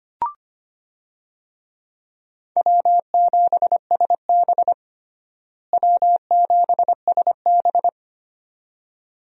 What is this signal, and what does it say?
Morse code, heard as a keyed tone, about 25 wpm, 700 Hz. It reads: W7SB W7SB
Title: USA Call Signs - 4 Characters - Repeated 2x - 25wpm